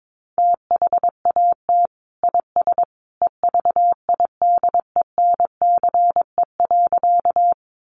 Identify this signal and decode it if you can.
Morse code, heard as a keyed tone, about 22 wpm, 700 Hz. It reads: THAT IS EVIDENCE.
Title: Sentences from Top 1300 Words 22wpm